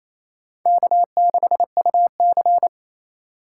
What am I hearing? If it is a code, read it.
Morse code, heard as a keyed tone, about 28 wpm, 700 Hz. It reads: K6UC
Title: USA Call Signs - 4 Characters - 3x Character Spacing - 28wpm